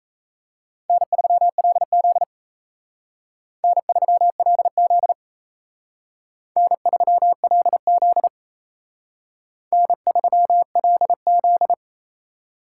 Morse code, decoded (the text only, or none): N3LZ N3LZ N3LZ N3LZ